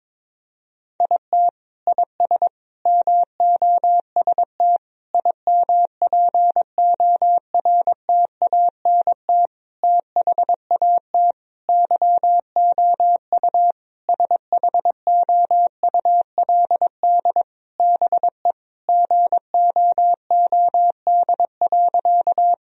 Morse code, heard as a keyed tone, about 22 wpm, 700 Hz. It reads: IT IS MOST IMPORTANT THAT YOU SHOULD BE GOOD.
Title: Sentences from Top 1000 Words - Review Part 1 - 22wpm